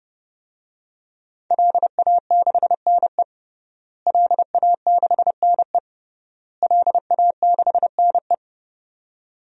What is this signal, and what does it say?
Morse code, heard as a keyed tone, about 30 wpm, 700 Hz. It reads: LA6NE LA6NE LA6NE
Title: DXCC-100 Call Signs - 5 Characters - Repeated 3x - 30wpm